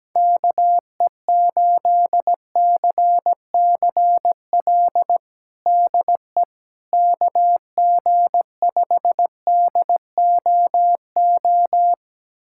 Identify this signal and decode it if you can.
Morse code, heard as a keyed tone, about 17 wpm, 700 Hz. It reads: KE8CCL DE KG5DOO
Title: USA Callsign DE Callsign - 6 Characters - 17wpm